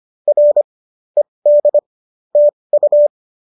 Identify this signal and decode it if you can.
Morse code, heard as a keyed tone, about 25 wpm, spaced as for 20 wpm, 570 Hz. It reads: R ED TU
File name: qso205 20